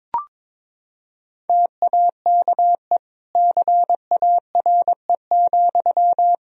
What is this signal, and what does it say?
Morse code, heard as a keyed tone, about 22 wpm, 700 Hz. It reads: TAKE CARE,
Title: Sentences from Top 1000 Words - Review Part 4 - 22wpm